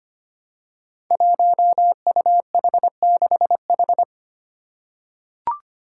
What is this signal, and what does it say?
Morse code, heard as a keyed tone, about 25 wpm, 700 Hz. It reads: 1UH6H
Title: Five Challenging Letter-Number Combinations 25wpm